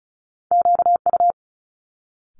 Morse code, heard as a keyed tone, about 36 wpm, 700 Hz. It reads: QU
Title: Multi-letter Phonemes Quick Fire 35wpm